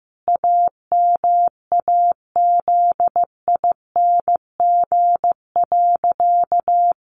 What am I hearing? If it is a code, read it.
Morse code, heard as a keyed tone, about 15 wpm, 700 Hz. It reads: AMAZING.
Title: Sentences from Top 1500 Words 15wpm